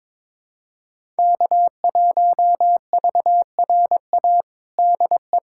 Morse code, heard as a keyed tone, about 21 wpm, 700 Hz. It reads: K1VRA DE
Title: USA Callsign DE Callsign - 5 Characters - 22wpm